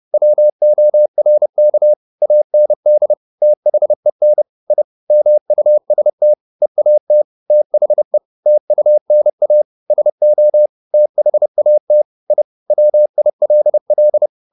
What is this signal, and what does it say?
Morse code, heard as a keyed tone, about 30 wpm, 600 Hz. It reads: WORK AND THEN I MUST EAT THE TUNA SO THAT I WILL
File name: the old man and the sea / the old man and the sea 1 30wpm